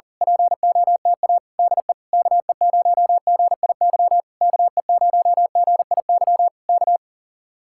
Morse code, heard as a keyed tone, about 38 wpm, 700 Hz. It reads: POTA DE KE0GIY KE0GIY K
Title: Speed-Racing Common QSO Phrases Encore - 40wpm